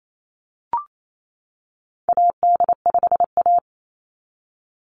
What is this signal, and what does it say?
Morse code, heard as a keyed tone, about 28 wpm, 700 Hz. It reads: AD5A
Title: NA SOTA Activator Callsigns 28wpm